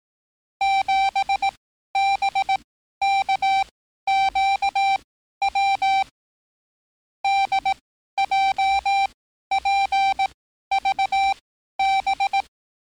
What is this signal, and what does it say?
Morse code, heard as a keyed tone, about 18 wpm, spaced as for 12 wpm, 780 Hz. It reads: 7BKQW DJPVB
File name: training / NAVPERS 11049to11060 / NAVPERS 11054 RA RB